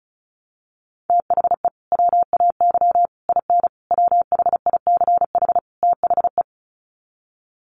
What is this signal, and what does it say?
Morse code, heard as a keyed tone, about 34 wpm, 700 Hz. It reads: THE WAY IN WHICH THE
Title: Sets of 5 Words Encore2 35wpm